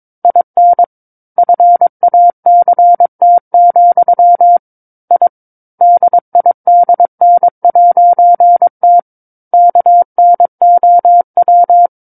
Morse code, heard as a keyed tone, about 22 wpm, 700 Hz. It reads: IN FACT, I DIDN'T KNOW
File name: 06-Chapter-06 22 WPM